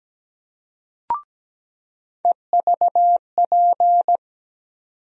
Morse code, heard as a keyed tone, about 17 wpm, 700 Hz. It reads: EVP
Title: USA Call Sign Postfixes - 3 Letters 17wpm